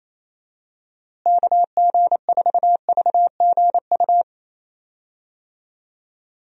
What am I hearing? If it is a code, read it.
Morse code, heard as a keyed tone, about 28 wpm, 700 Hz. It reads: KG4VGU
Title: USA Call Signs - 6 Characters - Repeated 2x - 28wpm